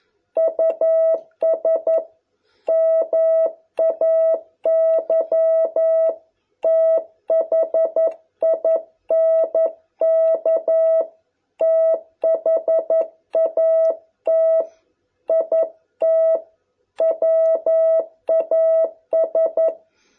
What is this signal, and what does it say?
Morse code, heard as a keyed tone, about 11 wpm, 660 Hz. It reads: US MAY THINK THAT IT WAS